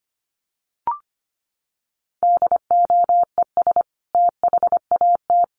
Morse code, heard as a keyed tone, about 25 wpm, 700 Hz. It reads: DOES THAT
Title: Sentences from Top 1300 Words 25wpm